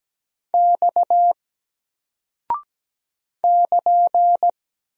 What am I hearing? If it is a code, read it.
Morse code, heard as a keyed tone, about 17 wpm, 700 Hz. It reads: X <KN>
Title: Single Character (Letter:Number:Punctuation:Prosign) Alt 17wpm